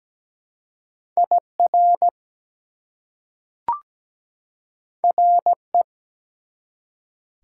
Morse code, heard as a keyed tone, about 17 wpm, 700 Hz. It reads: IR RE
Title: Prefixes and Suffixes 17wpm